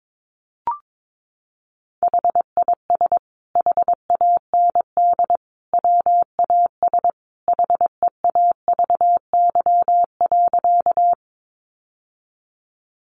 Morse code, heard as a keyed tone, about 22 wpm, 700 Hz. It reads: HIS HAND WAS HEAVY.
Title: Sentences from Top 1100 Words 22wpm